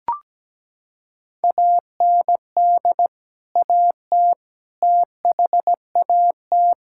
Morse code, heard as a keyed tone, about 17 wpm, 700 Hz. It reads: AND AT THAT
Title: Sentences from Top 1000 Words - Review Part 5 - 17wpm